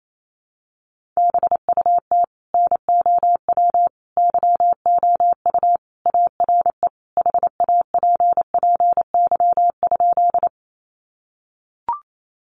Morse code, heard as a keyed tone, about 28 wpm, 700 Hz. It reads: BUT NOW YOU ARE HAPPY?
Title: Sentences from Top 1000 Words 28wpm